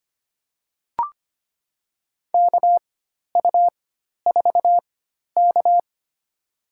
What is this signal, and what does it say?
Morse code, heard as a keyed tone, about 25 wpm, spaced as for 12 wpm, 700 Hz. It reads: KU4K
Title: USA Call Signs - 4 Characters - 4x Character Spacing - 25wpm